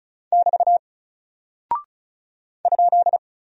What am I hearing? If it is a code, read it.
Morse code, heard as a keyed tone, about 36 wpm, 700 Hz. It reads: <BT> ?
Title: Single Character (Letter:Number:Punctuation:Prosign) Alt 35wpm